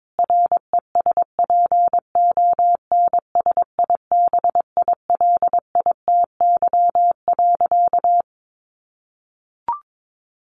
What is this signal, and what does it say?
Morse code, heard as a keyed tone, about 22 wpm, 700 Hz. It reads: RESPONSIBILITY.
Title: Sentences from Top 1200 Words 22wpm